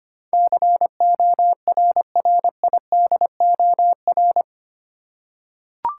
Morse code, heard as a keyed tone, about 25 wpm, 700 Hz. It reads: CORRIDOR
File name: Eight Letter Words - 25wpm